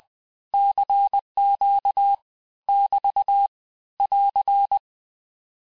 Morse code, heard as a keyed tone, about 20 wpm, 800 Hz. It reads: CQ <BT> <AR>
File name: cours-du-lundi files / cours-1584-texte-abreviations-vitesse-19mpm-puis-24mpm